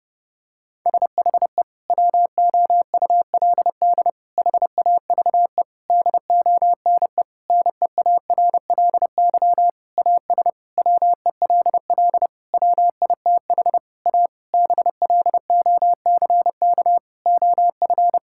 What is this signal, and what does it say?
Morse code, heard as a keyed tone, about 30 wpm, 700 Hz. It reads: SHE WOULD HAVE DONE NEARLY AS WELL WITH A BLOCK OF
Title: Sentences from Top 1600 Words 30wpm